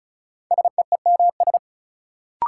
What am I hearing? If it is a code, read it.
Morse code, heard as a keyed tone, about 34 wpm, 700 Hz. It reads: SEEMS
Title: Sets of 3 Words - New Word Drill - 35wpm